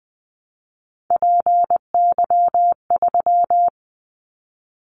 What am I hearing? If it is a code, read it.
Morse code, heard as a keyed tone, about 20 wpm, 700 Hz. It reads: PY3
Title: Callsign Prefixes for Top 50 DXCC - 20wpm